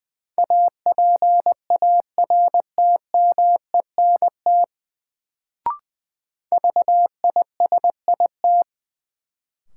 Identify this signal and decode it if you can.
Morse code, heard as a keyed tone, about 20 wpm, 700 Hz. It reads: APARTMENT VISIT
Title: Sentences from Top 900 Words - New Word Drill - 20wpm